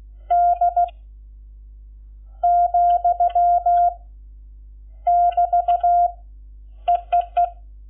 Morse code, heard as a keyed tone, about 16 wpm, 700 Hz. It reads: D , <BT> EEE
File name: Lecciones / CLASE Nº11